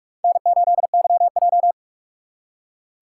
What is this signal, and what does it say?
Morse code, heard as a keyed tone, about 59 wpm, 700 Hz. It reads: N8YJ